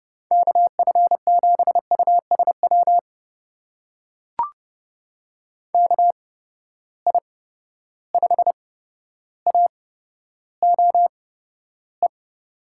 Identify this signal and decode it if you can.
Morse code, heard as a keyed tone, about 28 wpm, spaced as for 13 wpm, 700 Hz. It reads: KF7USW KI5AOE
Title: USA Call Signs - 6 Characters - 8x Character Spacing - 30wpm